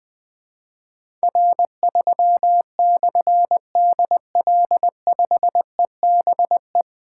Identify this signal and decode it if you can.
Morse code, heard as a keyed tone, about 21 wpm, 700 Hz. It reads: R3/DL5EBE
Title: Difficult Call Signs 20wpm